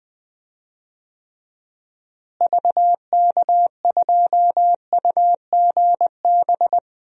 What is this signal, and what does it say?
Morse code, heard as a keyed tone, about 20 wpm, 700 Hz. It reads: VK2UGB
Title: DXCC-100 Call Signs - 6 Characters - Repeated 3x - 20wpm